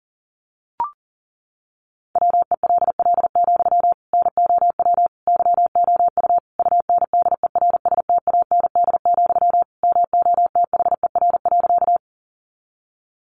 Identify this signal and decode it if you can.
Morse code, heard as a keyed tone, about 40 wpm, 700 Hz. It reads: WELL, NOW YOU UNDERSTAND, MOTHER.
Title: Sentences from Top 600 Words 40wpm